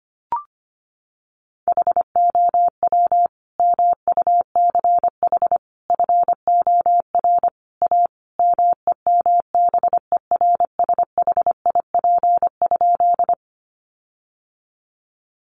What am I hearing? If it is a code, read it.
Morse code, heard as a keyed tone, about 25 wpm, 700 Hz. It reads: HOW MUCH FOR A MEMBERSHIP?
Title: Sentences from Top 1400 Words 25wpm